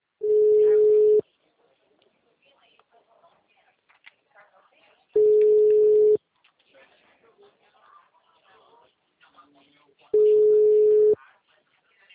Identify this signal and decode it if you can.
Ringback tone (Europe, 425 Hz), with speech.